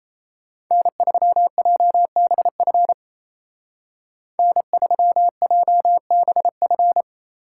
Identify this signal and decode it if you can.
Morse code, heard as a keyed tone, about 37 wpm, 700 Hz. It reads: N3JBF N3JBF